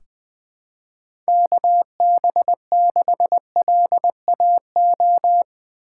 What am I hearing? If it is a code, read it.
Morse code, heard as a keyed tone, about 20 wpm, 700 Hz. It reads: KB6LAO
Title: USA Call Signs - 6 Characters - 8x Character Spacing - 20wpm